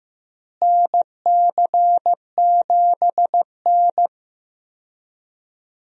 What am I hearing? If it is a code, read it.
Morse code, heard as a keyed tone, about 16 wpm, 700 Hz. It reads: NC7N
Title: USA Call Signs - 4 Characters - Repeated 3x - 15wpm